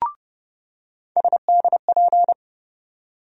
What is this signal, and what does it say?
Morse code, heard as a keyed tone, about 30 wpm, 700 Hz. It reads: SDP
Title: USA Call Sign Postfixes - 3 Letters 30wpm